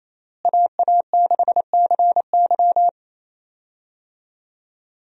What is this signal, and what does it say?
Morse code, heard as a keyed tone, about 27 wpm, 700 Hz. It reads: AA6CY
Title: USA Call Signs - 5 Characters - Repeated 3x - 28wpm